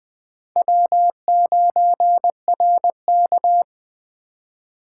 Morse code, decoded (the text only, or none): W9RK